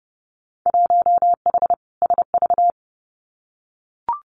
Morse code, heard as a keyed tone, about 30 wpm, 700 Hz. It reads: 1H SV
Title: Winter Field Day Exchanges 30wpm